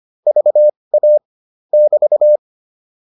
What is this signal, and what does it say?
Morse code, heard as a keyed tone, about 25 wpm, spaced as for 20 wpm, 600 Hz. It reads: VA <BT>